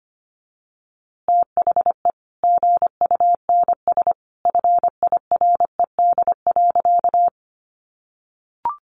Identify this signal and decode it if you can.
Morse code, heard as a keyed tone, about 25 wpm, 700 Hz. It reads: THE GUNS FIRED.
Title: A Sound Of Thunder 25wpm